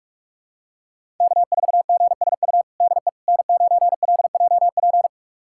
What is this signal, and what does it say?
Morse code, heard as a keyed tone, about 46 wpm, 700 Hz. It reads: K4GSU DE N9LJP
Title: USA Callsign DE Callsign 45wpm